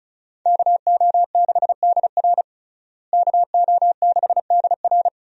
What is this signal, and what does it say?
Morse code, heard as a keyed tone, about 36 wpm, 700 Hz. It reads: KO6DR KO6DR
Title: USA Call Signs - 5 Characters - Repeated 2x - 35wpm